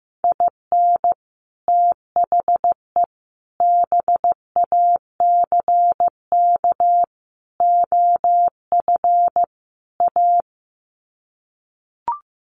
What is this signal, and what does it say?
Morse code, heard as a keyed tone, about 15 wpm, 700 Hz. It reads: IN THE BACK OF A
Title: Sets of 5 Words Encore2 15wpm